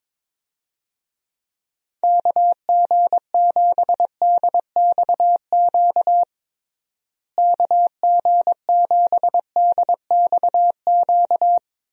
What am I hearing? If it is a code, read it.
Morse code, heard as a keyed tone, about 22 wpm, 700 Hz. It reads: KG7DXQ KG7DXQ
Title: USA Call Signs - 6 Characters - Repeated 2x - 22wpm